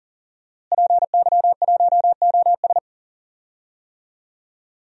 Morse code, heard as a keyed tone, about 40 wpm, 700 Hz. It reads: PY1OS
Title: DXCC-100 Call Signs - 5 Characters - Repeated 3x - 40wpm